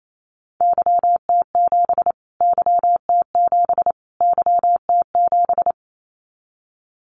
Morse code, heard as a keyed tone, about 27 wpm, 700 Hz. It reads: YT7 YT7 YT7
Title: Callsign Prefixes for Top 25 DXCC - Repeated 3x - 28wpm